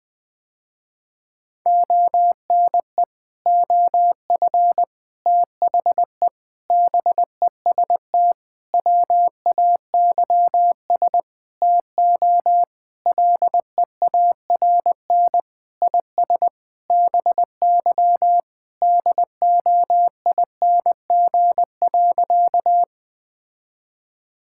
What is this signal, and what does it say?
Morse code, heard as a keyed tone, about 20 wpm, 700 Hz. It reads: ONE OF THE BEST WAYS TO LEARN IS BY DOING.
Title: Sentences with 6-Word Phrases - Spoken Beginning - 20wpm